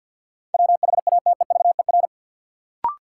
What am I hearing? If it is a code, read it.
Morse code, heard as a keyed tone, about 52 wpm, 700 Hz. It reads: WHATEVER
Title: Sentences from Top 1100 Words - New Word Drill Alt - 50wpm